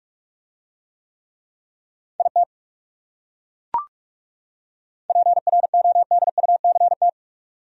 Morse code, heard as a keyed tone, about 40 wpm, 700 Hz. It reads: IT PRODUCT